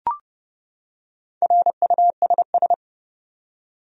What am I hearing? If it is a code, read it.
Morse code, heard as a keyed tone, about 31 wpm, 700 Hz. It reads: RUSS
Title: USA Common Names - Max 4 Letters - 30wpm